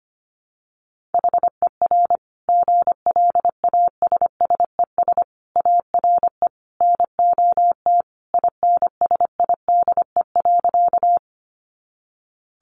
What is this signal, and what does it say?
Morse code, heard as a keyed tone, about 25 wpm, 700 Hz. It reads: HER GLASSES ARE NOT INSIDE.
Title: Additional Easy Phrases 25wpm